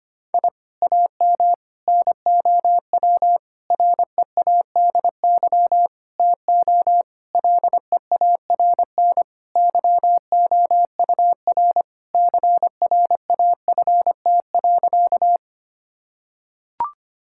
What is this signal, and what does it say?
Morse code, heard as a keyed tone, about 25 wpm, 700 Hz. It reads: I AM NOW READY TO LEARN YOUR CRAFT.
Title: Sentences from Top 1000 Words - Review Part 6 - 25wpm